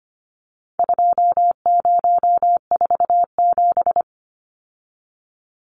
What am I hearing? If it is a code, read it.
Morse code, heard as a keyed tone, about 25 wpm, 700 Hz. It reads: 2047
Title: Numbers - 4-Digit 25wpm